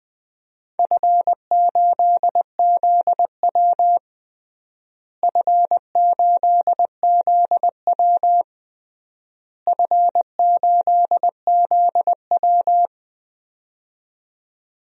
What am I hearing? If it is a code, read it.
Morse code, heard as a keyed tone, about 21 wpm, 700 Hz. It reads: F8ZW F8ZW F8ZW
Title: DXCC-100 Call Signs - 4 Characters - Repeated 3x - 20wpm